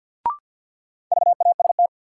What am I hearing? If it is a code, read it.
Morse code, heard as a keyed tone, about 53 wpm, 700 Hz. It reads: VAST
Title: Sentences from Top 1300 Words - New Word Drill Alt - 50wpm